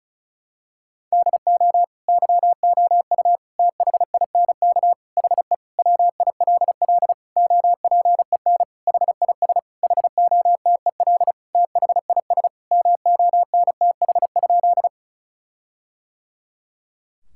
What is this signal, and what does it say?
Morse code, heard as a keyed tone, about 34 wpm, 700 Hz. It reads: DO YOU THINK HE WILL OPEN HIS HOTEL THIS MONTH?
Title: Sentences from Top 1600 Words 35wpm